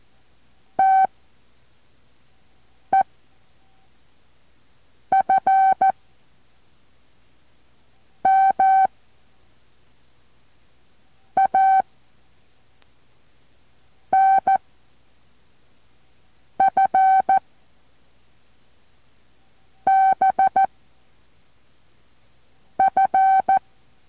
Morse code, heard as a keyed tone, about 14 wpm, 770 Hz. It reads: T E F M A N F B F
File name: intl morse code2